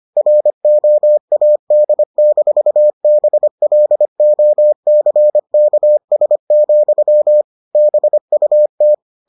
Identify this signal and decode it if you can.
Morse code, heard as a keyed tone, about 25 wpm, 600 Hz. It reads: ROAD-BLOCKS, BUT